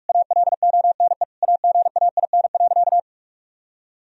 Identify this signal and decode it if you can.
Morse code, heard as a keyed tone, about 41 wpm, 700 Hz. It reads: ALONE AGAIN.
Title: Sentences from Top 1000 Words - Review Part 1 - 45wpm